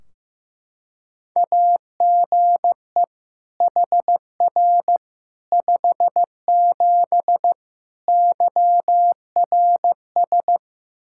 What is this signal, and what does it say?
Morse code, heard as a keyed tone, about 15 wpm, 700 Hz. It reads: AGE HR 57 YRS
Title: Practice QSO - Exchange 3 - More Info - 15wpm